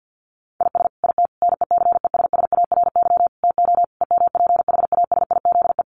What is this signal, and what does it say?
Morse code, heard as a keyed tone, about 50 wpm, 700 Hz. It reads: IS IT NECESSARY TO EMPHASIZE
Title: Sentences from Top 1500 Words 50wpm